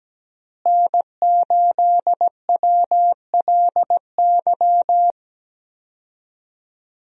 Morse code, heard as a keyed tone, about 17 wpm, 700 Hz. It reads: N8WLY
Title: USA Call Signs - 5 Characters - Repeated 3x - 17wpm